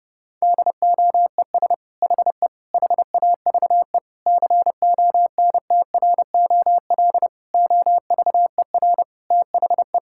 Morse code, heard as a keyed tone, about 30 wpm, 700 Hz. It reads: DOES HE HAVE CONTROL OVER THE
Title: Sentences from Top 1000 Words - Review Part 5 - 30wpm